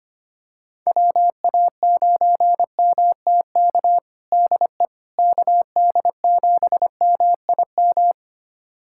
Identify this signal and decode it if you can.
Morse code, heard as a keyed tone, about 25 wpm, 700 Hz. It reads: WA9MTK DE KD7MIM
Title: USA Callsign DE Callsign - 6 Characters - 25wpm